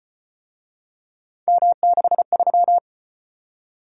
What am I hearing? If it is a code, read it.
Morse code, heard as a keyed tone, about 33 wpm, 700 Hz. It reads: M63